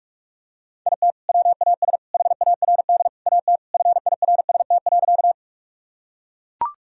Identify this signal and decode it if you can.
Morse code, heard as a keyed tone, about 41 wpm, 700 Hz. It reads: IT WAS HARD AT FIRST.
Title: Sentences from Top 1000 Words - Review Part 1 - 45wpm